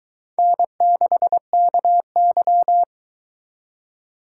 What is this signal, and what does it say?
Morse code, heard as a keyed tone, about 22 wpm, 700 Hz. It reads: N6KY